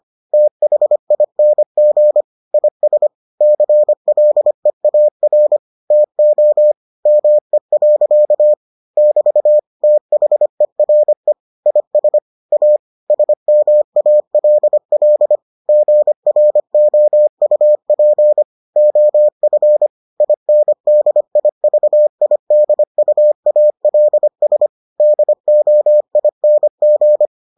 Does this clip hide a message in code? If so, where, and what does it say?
Yes: Morse code from 0.3 to 27.3 s, heard as a keyed tone, about 25 wpm, 600 Hz. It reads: THING IS CLEAR TO ME. <BT> THERE IS A SMALL GROUP OF INDIVIDUALS DOING